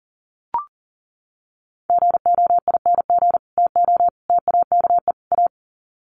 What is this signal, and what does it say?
Morse code, heard as a keyed tone, about 40 wpm, 700 Hz. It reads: GOING TO TAKE A
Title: Sets of 4 Words 40wpm